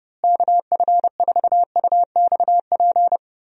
Morse code, heard as a keyed tone, about 32 wpm, 700 Hz. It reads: KF4UXP
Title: USA Call Signs - 6 Characters - Repeated 3x - 30wpm